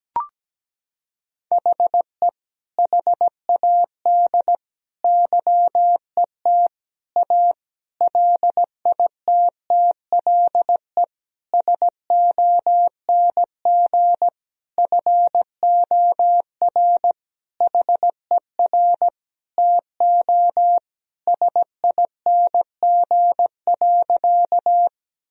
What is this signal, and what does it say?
Morse code, heard as a keyed tone, about 17 wpm, 700 Hz. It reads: HE HAD YET A LITTLE SONG FOR HER TO SING.
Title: Sentences from Top 1000 Words - Review Part 2 - 17wpm